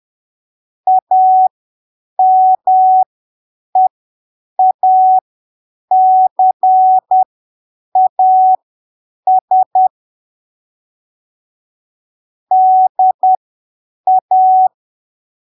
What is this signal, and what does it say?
Morse code, heard as a keyed tone, about 10 wpm, spaced as for 7 wpm, 750 Hz. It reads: AMEACAS DA
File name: arquivos / texto / texto 30 10ppm